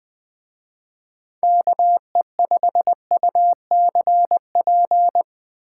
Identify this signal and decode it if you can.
Morse code, heard as a keyed tone, about 21 wpm, 700 Hz. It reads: KE5UCP